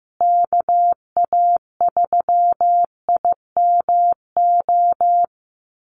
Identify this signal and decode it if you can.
Morse code, heard as a keyed tone, about 15 wpm, 700 Hz. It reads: KA3IMO
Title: USA Call Signs Repeated 3x 15wpm